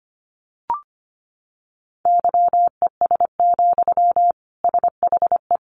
Morse code, heard as a keyed tone, about 25 wpm, 700 Hz. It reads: YES, SHE
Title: Sentences from Top 700 Words 25wpm